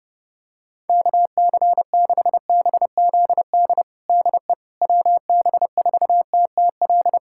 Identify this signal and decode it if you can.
Morse code, heard as a keyed tone, about 32 wpm, 700 Hz. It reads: KC6BZD DE WB4TTL
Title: USA Callsign DE Callsign - 6 Characters - 30wpm